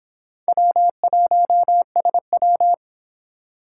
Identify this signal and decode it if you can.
Morse code, heard as a keyed tone, about 26 wpm, 700 Hz. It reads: W1SW